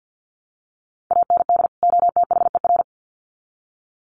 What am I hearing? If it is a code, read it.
Morse code, heard as a keyed tone, about 50 wpm, 700 Hz. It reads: AND OTHER
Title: Sets of 2 Words 50wpm